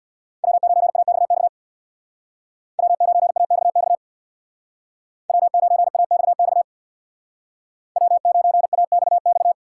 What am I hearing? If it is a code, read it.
Morse code, heard as a keyed tone, about 63 wpm, 700 Hz. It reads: W9AXX W9AXX W9AXX W9AXX